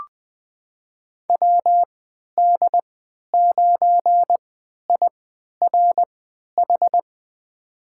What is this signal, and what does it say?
Morse code, heard as a keyed tone, about 20 wpm, spaced as for 11 wpm, 700 Hz. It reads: WD9IRH
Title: USA Call Signs - 3x Character Spacing - 20wpm